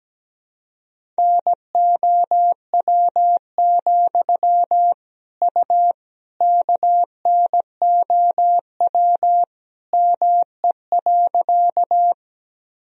Morse code, heard as a keyed tone, about 17 wpm, 700 Hz. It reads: NOW, U KNOW ME.
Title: Sentences from Top 200 Words and Abbreviations 17wpm